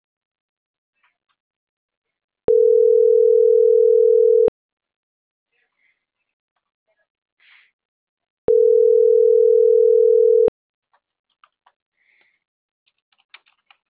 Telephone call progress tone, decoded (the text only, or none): ringback tone